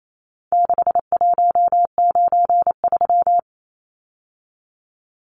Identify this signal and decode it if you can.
Morse code, heard as a keyed tone, about 28 wpm, 700 Hz. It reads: 6193
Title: Numbers - 4-Digit 28wpm